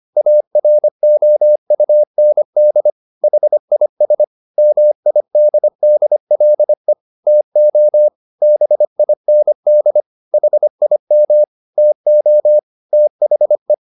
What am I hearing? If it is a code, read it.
Morse code, heard as a keyed tone, about 25 wpm, 600 Hz. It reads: AROUND HIS MIDDLE TO BIND HIM TO THE